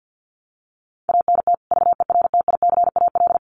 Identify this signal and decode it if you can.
Morse code, heard as a keyed tone, about 50 wpm, 700 Hz. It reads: ANT VERTICAL
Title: Common QSO Phrases 50wpm